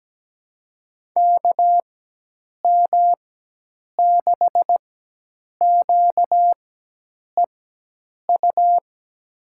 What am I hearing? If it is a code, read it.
Morse code, heard as a keyed tone, about 17 wpm, spaced as for 8 wpm, 700 Hz. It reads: KM6QEU
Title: USA Call Signs - 6 Characters - 4x Character Spacing - 17wpm